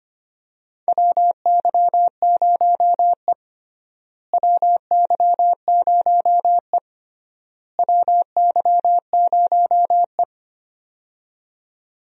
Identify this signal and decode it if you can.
Morse code, heard as a keyed tone, about 25 wpm, 700 Hz. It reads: WY0E WY0E WY0E
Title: USA Call Signs - 4 Characters - Repeated 3x - 25wpm